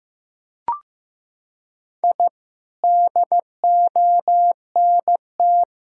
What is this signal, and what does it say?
Morse code, heard as a keyed tone, about 15 wpm, 700 Hz. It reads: I DONT
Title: Sentences from Top 1000 Words - Review Part 6 - 15wpm